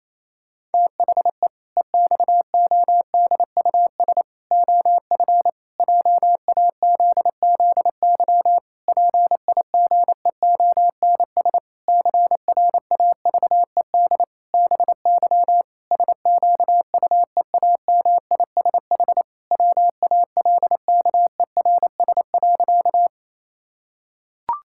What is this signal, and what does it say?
Morse code, heard as a keyed tone, about 28 wpm, 700 Hz. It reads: THE EXODUS OF JAZZY PIGEONS CRAVED BY SQUEAMISH WALKERS.
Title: Pangram Challenge v2 - 28wpm